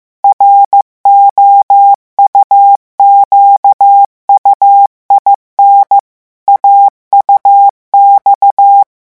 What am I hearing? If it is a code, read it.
Morse code, heard as a keyed tone, about 15 wpm, 800 Hz. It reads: ROUQUIN AUX
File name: morse mp3 / revision9 citations humoristiques vitesse 14 mots minute